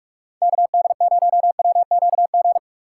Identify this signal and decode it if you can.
Morse code, heard as a keyed tone, about 46 wpm, 700 Hz. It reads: KD0WQG
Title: USA Call Signs - 6 Characters - 6x Character Spacing - 45wpm